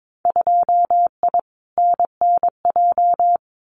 Morse code, heard as a keyed tone, about 22 wpm, 700 Hz. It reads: 2I NNJ
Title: Winter Field Day Exchanges Encore 22wpm